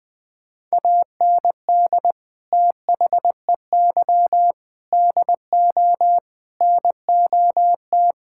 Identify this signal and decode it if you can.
Morse code, heard as a keyed tone, about 21 wpm, 700 Hz. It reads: AND THEY DO NOT
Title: Sentences with 4-Word Phrases - Spoken Beginning - 20wpm